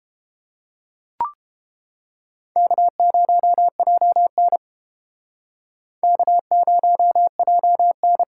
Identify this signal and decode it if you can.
Morse code, heard as a keyed tone, about 31 wpm, 700 Hz. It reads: K0JN K0JN